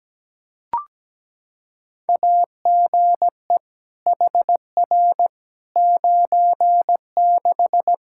Morse code, heard as a keyed tone, about 17 wpm, 700 Hz. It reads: AGE HR 96
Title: K1USN SST QSO Practice 17wpm